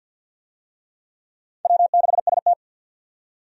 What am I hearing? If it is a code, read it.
Morse code, heard as a keyed tone, about 48 wpm, 700 Hz. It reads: W6ST
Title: USA Call Signs - 4 Characters - 50wpm